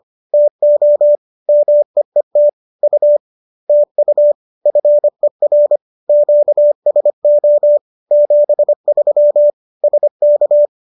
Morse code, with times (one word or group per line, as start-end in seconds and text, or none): 0.3-1.1 TO
1.5-2.5 MEET
2.8-3.2 U
3.7-4.3 TU
4.7-5.8 FER
6.1-7.8 QSO
8.1-9.5 73
9.8-10.7 SK